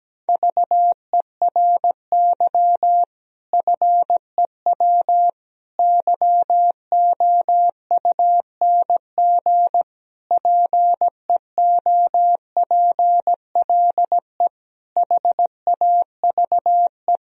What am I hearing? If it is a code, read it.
Morse code, heard as a keyed tone, about 17 wpm, 700 Hz. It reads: VERY FEW YOUNG PEOPLE HAVE
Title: Sentences from Top 1000 Words - Review Part 1 - 17wpm